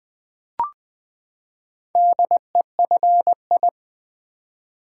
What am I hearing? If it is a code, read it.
Morse code, heard as a keyed tone, about 20 wpm, 700 Hz. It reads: DEFI
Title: Build A Word - 4 Letters At A Time - 10-Letter Words - 20wpm